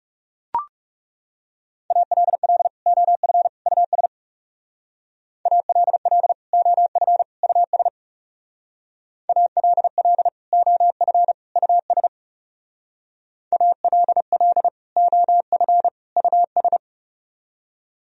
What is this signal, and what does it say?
Morse code, heard as a keyed tone, about 37 wpm, 700 Hz. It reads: ALL OF US ALL OF US ALL OF US ALL OF US
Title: Speed-Racing - Sets of 3 Words 30wpm